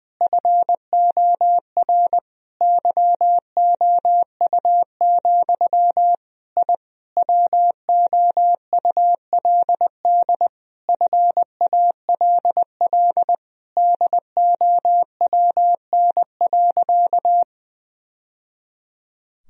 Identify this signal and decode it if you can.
Morse code, heard as a keyed tone, about 20 wpm, 700 Hz. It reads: FOR YOU, I WOULD FALL DOWN.
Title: Sentences from Top 1000 Words - Review Part 2 - 20wpm